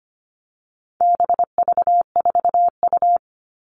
Morse code, heard as a keyed tone, about 25 wpm, 700 Hz. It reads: BV4U
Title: All Cut Numbers - 4-Digit 25wpm